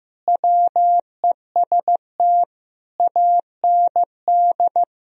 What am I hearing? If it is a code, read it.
Morse code, heard as a keyed tone, about 15 wpm, 700 Hz. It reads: WEST AND
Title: Sentences from Top 1000 Words - Review Part 2 - 15wpm